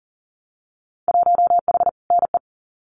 Morse code, heard as a keyed tone, about 39 wpm, 700 Hz. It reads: 1H NE
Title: Winter Field Day Exchanges Encore 40wpm